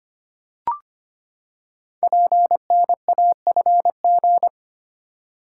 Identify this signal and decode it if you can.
Morse code, heard as a keyed tone, about 25 wpm, 700 Hz. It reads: PNAFG
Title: Code Groups - 5 Letters - 25wpm